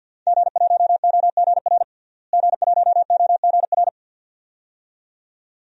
Morse code, heard as a keyed tone, about 48 wpm, 700 Hz. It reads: G1OGR G1OGR
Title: DXCC-100 Call Signs - 5 Characters - Repeated 2x - 50wpm